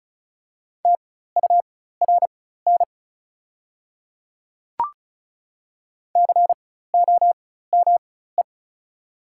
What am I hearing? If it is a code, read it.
Morse code, heard as a keyed tone, about 36 wpm, spaced as for 16 wpm, 700 Hz. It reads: TURN COME
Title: 4x Character Spacing - Top 200 Words - 35wpm